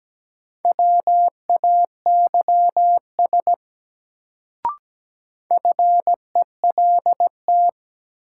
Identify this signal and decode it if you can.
Morse code, heard as a keyed tone, about 17 wpm, 700 Hz. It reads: WAYS FELT
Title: Sets of 5 Words Encore - New Word Drill - 17wpm